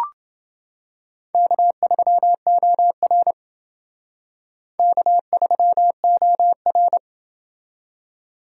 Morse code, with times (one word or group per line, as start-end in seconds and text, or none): 1.3-3.3 K3OR
4.8-7.0 K3OR